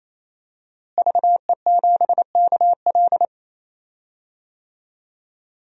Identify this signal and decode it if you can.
Morse code, heard as a keyed tone, about 28 wpm, 700 Hz. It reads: VE7KL
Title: DXCC-100 Call Signs - 5 Characters - Repeated 3x - 28wpm